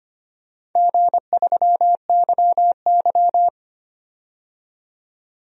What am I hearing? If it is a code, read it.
Morse code, heard as a keyed tone, about 25 wpm, 700 Hz. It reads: G3YY
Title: DXCC-100 Call Signs - 4 Characters - 25wpm